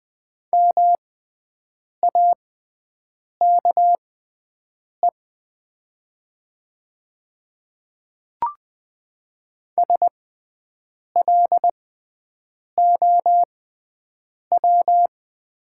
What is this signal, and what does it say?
Morse code, heard as a keyed tone, about 20 wpm, spaced as for 7 wpm, 690 Hz. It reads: MAKE SLOW
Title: 6x Character Spacing - Top 500 Words - 20wpm